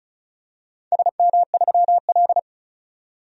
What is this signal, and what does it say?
Morse code, heard as a keyed tone, about 34 wpm, 700 Hz. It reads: SM3L
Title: DXCC-100 Call Signs - 4 Characters - 35wpm